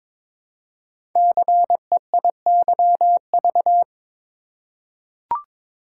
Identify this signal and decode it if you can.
Morse code, heard as a keyed tone, about 22 wpm, 700 Hz. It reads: CEIYV
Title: Code Groups - 5 Letter:Numbers - 22wpm